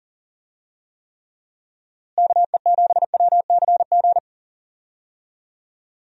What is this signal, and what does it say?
Morse code, heard as a keyed tone, about 37 wpm, 700 Hz. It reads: KE7WCG
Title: USA Call Signs - 6 Characters - Repeated 3x - 40wpm